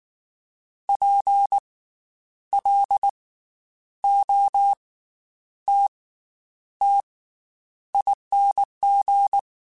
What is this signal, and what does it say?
Morse code, heard as a keyed tone, about 19 wpm, 800 Hz. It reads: P L O T T ING
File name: Special INGs / Special INGs-mp3s / Slow ing @18